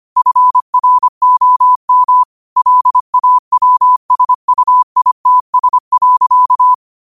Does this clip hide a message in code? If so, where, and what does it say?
Yes: Morse code from 0.2 to 6.7 s, heard as a keyed tone, about 25 wpm, 1000 Hz. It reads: FROM LAWSUITS.